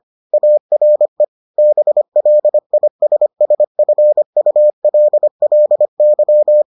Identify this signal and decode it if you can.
Morse code, heard as a keyed tone, about 25 wpm, 600 Hz. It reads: ARE BLISSFULLY